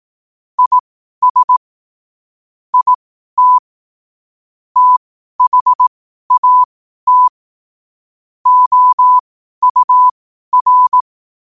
Morse code, heard as a keyed tone, about 18 wpm, spaced as for 12 wpm, 1000 Hz. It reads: IS IT THAT OUR
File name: qotdfiles / 12wpm